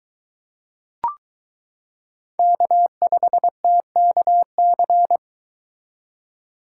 Morse code, heard as a keyed tone, about 22 wpm, 700 Hz. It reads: K5TKC